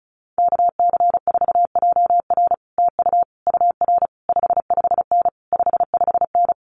Morse code, heard as a keyed tone, about 36 wpm, 700 Hz. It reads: KC4JR TU UR 55N 55N
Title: Common QSO Phrases Encore 35wpm